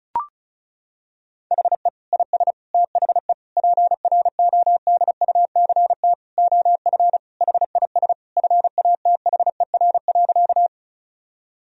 Morse code, heard as a keyed tone, about 34 wpm, 700 Hz. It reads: HE IS THE PRODUCT OF HIS FATHER.
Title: Sentences from Top 1000 Words - Review Part 2 - 35wpm